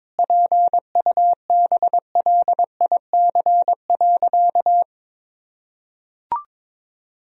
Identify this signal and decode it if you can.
Morse code, heard as a keyed tone, about 22 wpm, 700 Hz. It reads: PUBLIC.
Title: Sentences from Top 1000 Words - Review Part 3 - 22wpm